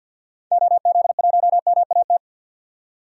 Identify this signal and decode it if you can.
Morse code, heard as a keyed tone, about 53 wpm, 700 Hz. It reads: OZ1MAT
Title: DXCC-100 Call Signs - 6 Characters - 50wpm